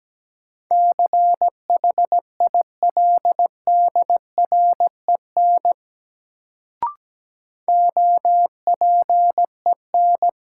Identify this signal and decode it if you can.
Morse code, heard as a keyed tone, about 17 wpm, 700 Hz. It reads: CHILDREN OPEN
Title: Sentences from Top 300 Words - New Word Drill - 17wpm